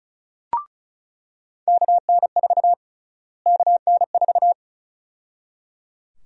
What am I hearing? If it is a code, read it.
Morse code, heard as a keyed tone, about 34 wpm, 700 Hz. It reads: KN4 KN4
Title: USA Call Sign Prefixes - Repeated 2x - 35wpm